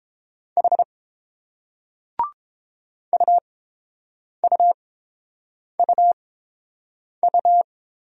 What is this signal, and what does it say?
Morse code, heard as a keyed tone, about 30 wpm, spaced as for 7 wpm, 700 Hz. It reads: H UUUU